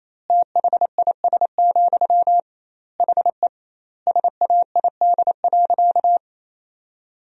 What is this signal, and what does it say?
Morse code, heard as a keyed tone, about 27 wpm, 700 Hz. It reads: THIS, HE SAID.
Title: 2x Word-Spacing - Sentences from Top 400 Words - 28wpm